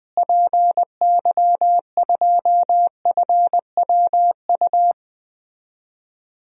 Morse code, heard as a keyed tone, about 20 wpm, 700 Hz. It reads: PY2FWU
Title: DXCC-100 Call Signs - 6 Characters - Repeated 2x - 20wpm